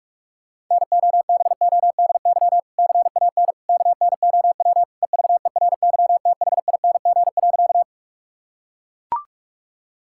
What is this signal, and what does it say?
Morse code, heard as a keyed tone, about 42 wpm, 700 Hz. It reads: NOBODY CAN KNOW EVERYTHING.
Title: Sentences from Top 1600 Words 45wpm